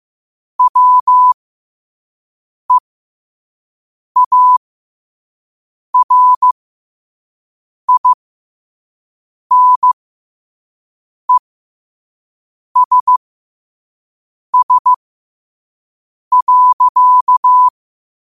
Morse code, heard as a keyed tone, about 15 wpm, spaced as for 5 wpm, 1000 Hz. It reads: WEARINESS.
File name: qotdfiles / 5wpm